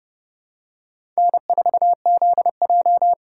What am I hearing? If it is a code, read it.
Morse code, heard as a keyed tone, about 30 wpm, 700 Hz. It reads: N4ZJ
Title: USA Call Signs - 3x Character Spacing - 30wpm